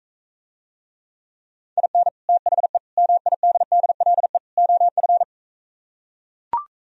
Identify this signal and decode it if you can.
Morse code, heard as a keyed tone, about 44 wpm, 700 Hz. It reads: IN THE MIDDLE OF